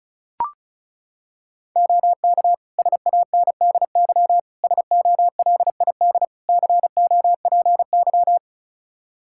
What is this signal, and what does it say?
Morse code, heard as a keyed tone, about 36 wpm, 700 Hz. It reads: OK SANDY SOLID COPY
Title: K1USN SST QSO Practice 35wpm